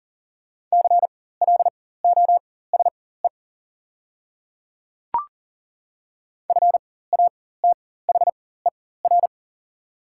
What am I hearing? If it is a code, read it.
Morse code, heard as a keyed tone, about 39 wpm, spaced as for 19 wpm, 700 Hz. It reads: CLOSE FATHER
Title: 4x Character Spacing - Top 300 Words - 40wpm